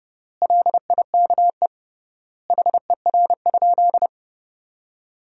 Morse code, heard as a keyed tone, about 30 wpm, 700 Hz. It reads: LIKE HER?
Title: 3x Word-Spacing - Sentences from Top 100 Words - 30wpm